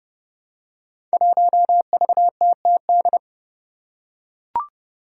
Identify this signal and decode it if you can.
Morse code, heard as a keyed tone, about 32 wpm, 700 Hz. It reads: 1VTTD
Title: Five Challenging Letter-Number Combinations 30wpm